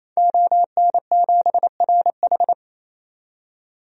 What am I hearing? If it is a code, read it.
Morse code, heard as a keyed tone, about 28 wpm, 700 Hz. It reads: ON7RH
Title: DXCC-100 Call Signs - 5 Characters - 28wpm